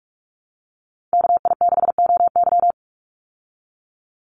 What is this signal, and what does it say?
Morse code, heard as a keyed tone, about 44 wpm, 700 Hz. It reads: KI6OY
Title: Call Signs Repeated 3x - 45wpm